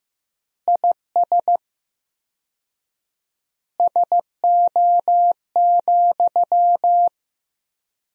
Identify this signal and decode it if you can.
Morse code, heard as a keyed tone, about 15 wpm, 700 Hz. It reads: IS SO,
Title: 4x Word-Spacing - Sentences from Top 200 Words - 15wpm